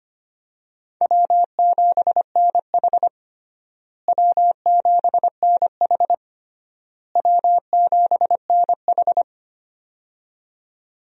Morse code, heard as a keyed tone, about 25 wpm, 700 Hz. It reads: W7NH W7NH W7NH
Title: USA Call Signs - 4 Characters - Repeated 3x - 25wpm